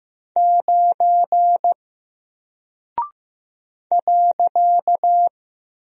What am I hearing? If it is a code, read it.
Morse code, heard as a keyed tone, about 15 wpm, 700 Hz. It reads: 9 .
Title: Single Character (Letter:Number:Punctuation:Prosign) Alt 15wpm